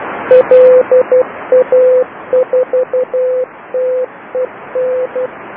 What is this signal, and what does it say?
Morse code, heard as a keyed tone, about 12 wpm, 510 Hz. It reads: LA4TEN
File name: LA4TEN beacon 1008 28210 Doucy sito2